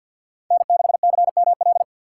Morse code, heard as a keyed tone, about 53 wpm, 700 Hz. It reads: N6CML
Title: USA Call Signs - 4x Character Spacing - 50wpm